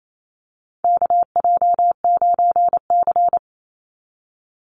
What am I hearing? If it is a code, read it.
Morse code, heard as a keyed tone, about 28 wpm, 700 Hz. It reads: KJ9C
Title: Call Signs - 28wpm